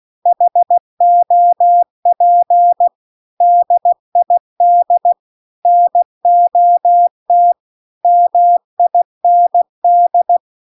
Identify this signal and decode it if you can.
Morse code, heard as a keyed tone, about 16 wpm, 700 Hz. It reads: HOP DID NOT MIND